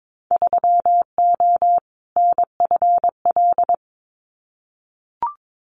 Morse code, heard as a keyed tone, about 22 wpm, 700 Hz. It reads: 3O NFL
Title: Winter Field Day Exchanges 22wpm